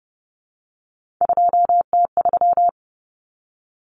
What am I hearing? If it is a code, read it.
Morse code, heard as a keyed tone, about 30 wpm, 700 Hz. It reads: 2T3
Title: Common Cut Numbers - 3-Digit 30wpm